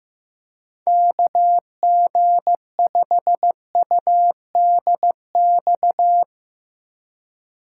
Morse code, heard as a keyed tone, about 16 wpm, 700 Hz. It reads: KG5UDX
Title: USA Call Signs - 3x Character Spacing - 15wpm